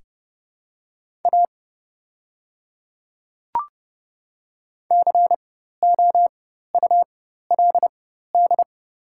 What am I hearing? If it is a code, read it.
Morse code, heard as a keyed tone, about 31 wpm, spaced as for 14 wpm, 700 Hz. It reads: A COULD
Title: 4x Character Spacing - Top 100 Words - 30wpm